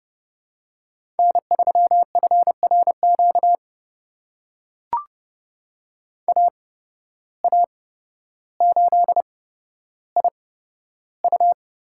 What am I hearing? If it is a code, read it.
Morse code, heard as a keyed tone, about 31 wpm, spaced as for 13 wpm, 700 Hz. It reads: N3FRQ AA8IU
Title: USA Call Signs - 5 Characters - 8x Character Spacing - 30wpm